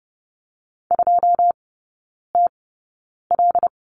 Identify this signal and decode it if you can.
Morse code, heard as a keyed tone, about 30 wpm, spaced as for 9 wpm, 690 Hz. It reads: 2TL
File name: Lesson 17e - Letter u - Three Characters 30wpm